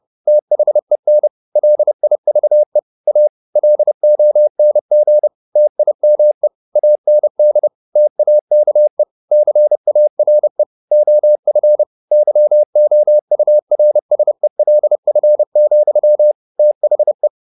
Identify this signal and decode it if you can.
Morse code, heard as a keyed tone, about 30 wpm, 600 Hz. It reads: THEN LIVE A LONG TIME AND TAKE CARE OF YOURSELF, THE
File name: the old man and the sea / the old man and the sea 1 30wpm